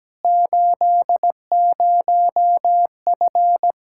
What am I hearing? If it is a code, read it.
Morse code, heard as a keyed tone, about 17 wpm, 700 Hz. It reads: 80F
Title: K1USN SST QSO Practice 17wpm